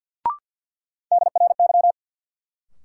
Morse code, heard as a keyed tone, about 51 wpm, 700 Hz. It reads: DRY
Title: Sentences from Top 500 Words - New Word Drill - 50wpm